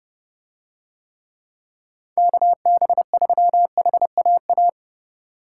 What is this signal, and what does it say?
Morse code, heard as a keyed tone, about 31 wpm, 700 Hz. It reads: KB3HAA